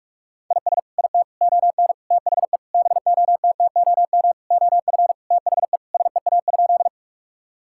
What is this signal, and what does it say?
Morse code, heard as a keyed tone, about 42 wpm, 700 Hz. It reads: IS IT ON THE BOTTOM OF THE SEA?
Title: Sentences from Top 1000 Words - Review Part 6 - 45wpm